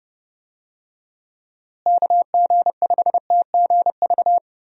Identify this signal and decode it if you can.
Morse code, heard as a keyed tone, about 30 wpm, 700 Hz. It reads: KG5TGV
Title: USA Call Signs - 6 Characters - Repeated 3x - 30wpm